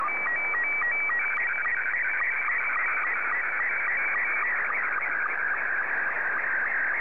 SSTV (slow-scan television), Scottie 2.
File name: sstv / S2-ZL2CC